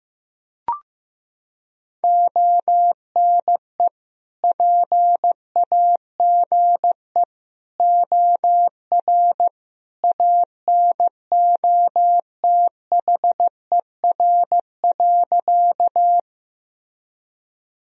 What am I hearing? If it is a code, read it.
Morse code, heard as a keyed tone, about 15 wpm, 700 Hz. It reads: ONE PAGE OR ANOTHER.
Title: Sentences from Top 1000 Words - Review Part 1 - 15wpm